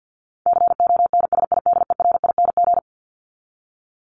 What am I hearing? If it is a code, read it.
Morse code, heard as a keyed tone, about 50 wpm, 700 Hz. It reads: CONSIDERING
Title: ING Suffix - Top 100 Words with Base Words 50wpm